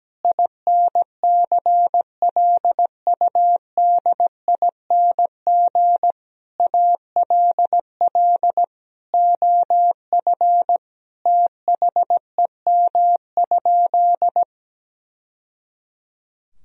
Morse code, heard as a keyed tone, about 17 wpm, 700 Hz. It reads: INCLUDING ALL OF THEM?
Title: Sentences from Top 1000 Words - Review Part 4 - 17wpm